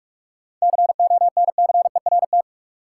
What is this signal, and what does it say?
Morse code, heard as a keyed tone, about 46 wpm, 700 Hz. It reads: CONCERT
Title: Seven Letter Words - 45wpm